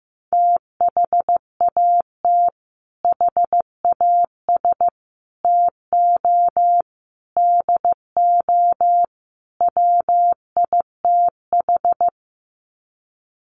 Morse code, heard as a keyed tone, about 15 wpm, 700 Hz. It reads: THAT HAS TO DO WITH
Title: Sets of 5 Words Encore2 15wpm